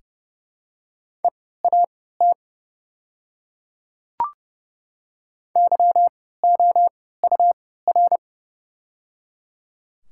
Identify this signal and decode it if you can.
Morse code, heard as a keyed tone, about 30 wpm, spaced as for 17 wpm, 690 Hz. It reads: EAT YOUR
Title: 3x Character Spacing - Top 500 Words - 30wpm